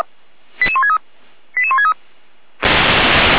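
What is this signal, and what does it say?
Five-tone selective call (ZVEI), heard as quick runs of five tones, tones of about 70 ms. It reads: 88163 88163
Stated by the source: Five-tone ZVEI selective call received on an SDR.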